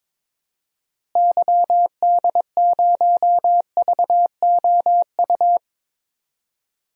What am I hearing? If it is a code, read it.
Morse code, heard as a keyed tone, about 22 wpm, 700 Hz. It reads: YD0VOU
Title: DXCC-100 Call Signs - 6 Characters - Repeated 3x - 22wpm